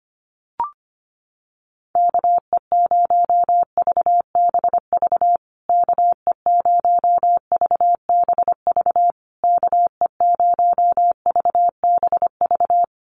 Morse code, heard as a keyed tone, about 25 wpm, 700 Hz. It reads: KE0VBV KE0VBV KE0VBV
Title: USA Call Signs Repeated 3x 25wpm